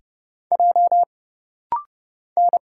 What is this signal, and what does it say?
Morse code, heard as a keyed tone, about 30 wpm, 700 Hz. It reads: J N
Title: Single Letter-Number Alt - Rapid-Fire 30wpm